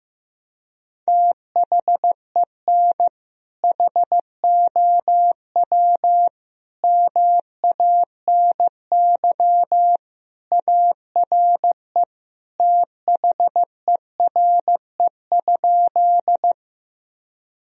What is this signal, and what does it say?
Morse code, heard as a keyed tone, about 15 wpm, 700 Hz. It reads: THEN HOW MANY ARE THERE?
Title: Sentences from Top 1000 Words - Review Part 1 - 15wpm